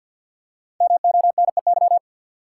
Morse code, heard as a keyed tone, about 48 wpm, 700 Hz. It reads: MONEY
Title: Five Letter Words - 50wpm